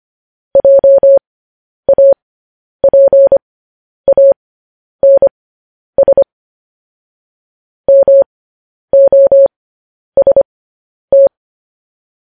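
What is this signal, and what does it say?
Morse code, heard as a keyed tone, about 25 wpm, spaced as for 10 wpm, 550 Hz. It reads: JAPANS MOST
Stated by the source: Morse code transmission of news headlines.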